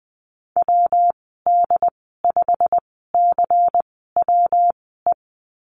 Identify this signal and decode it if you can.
Morse code, heard as a keyed tone, about 20 wpm, spaced as for 15 wpm, 700 Hz. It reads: WD5CWE
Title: 140 Lesson 23g - Letter f - Call Signs 20wpm